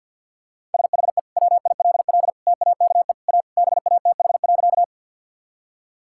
Morse code, heard as a keyed tone, about 50 wpm, 700 Hz. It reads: SHE WILL TAKE A BATH.
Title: Sentences from Top 1600 Words 50wpm